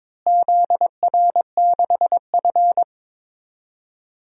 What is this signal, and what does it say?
Morse code, heard as a keyed tone, about 22 wpm, 700 Hz. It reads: ZR6F
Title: DXCC-100 Call Signs - 4 Characters - Repeated 3x - 22wpm